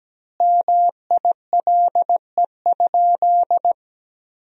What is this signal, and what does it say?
Morse code, heard as a keyed tone, about 17 wpm, 700 Hz. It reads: MILE?
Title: 3x Word-Spacing - Sentences from Top 400 Words - 17wpm